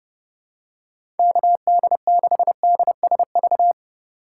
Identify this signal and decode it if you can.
Morse code, heard as a keyed tone, about 29 wpm, 700 Hz. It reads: KD6DSV
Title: USA Call Signs - 3x Character Spacing - 30wpm